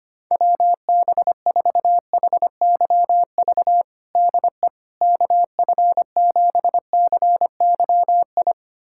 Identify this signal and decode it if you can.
Morse code, heard as a keyed tone, about 25 wpm, 700 Hz. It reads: WB4HYV DE KF7CYI
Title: USA Callsign DE Callsign - 6 Characters - 25wpm